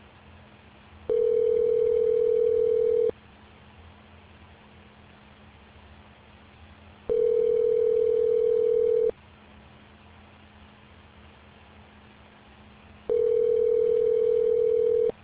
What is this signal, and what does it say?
Ringback tone (North America), 440 + 480 Hz.